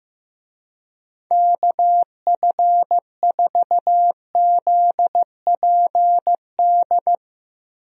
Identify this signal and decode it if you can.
Morse code, heard as a keyed tone, about 15 wpm, 700 Hz. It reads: KF4ZPD
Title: USA Call Signs - 6 Characters - 15wpm